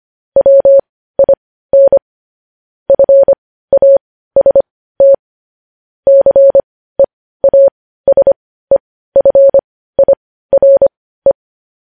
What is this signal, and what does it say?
Morse code, heard as a keyed tone, about 25 wpm, spaced as for 15 wpm, 550 Hz. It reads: WIN FAST CEASEFIRE
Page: news headlines in morse code at 15 wpm